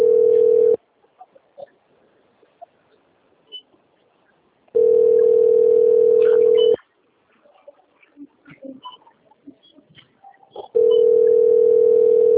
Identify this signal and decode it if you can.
Ringback tone (North America), 440 + 480 Hz.